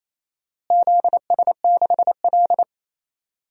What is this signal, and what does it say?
Morse code, heard as a keyed tone, about 28 wpm, 700 Hz. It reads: ZS6L
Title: DXCC-100 Call Signs - 4 Characters - Repeated 2x - 28wpm